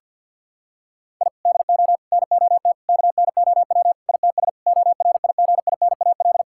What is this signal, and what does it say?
Morse code, heard as a keyed tone, about 50 wpm, 700 Hz. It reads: I DO NOT KNOW ITS ORIGINAL
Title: Sentences from Top 1000 Words - Review Part 6 - 50wpm